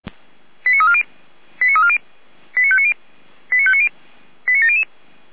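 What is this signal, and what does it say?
Five-tone selective call (ZVEI), heard as quick runs of five tones, tones of about 70 ms. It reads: 89340 89340 89590 89590 89700
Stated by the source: ZVEI five-tone call received with an SDR.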